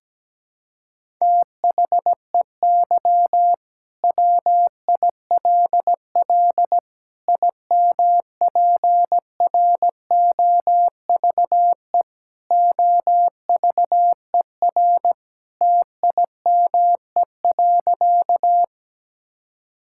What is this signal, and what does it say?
Morse code, heard as a keyed tone, about 17 wpm, 700 Hz. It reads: THEY WILL IMPROVE OVER TIME.
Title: Sentences from Top 1000 Words - Review Part 5 - 17wpm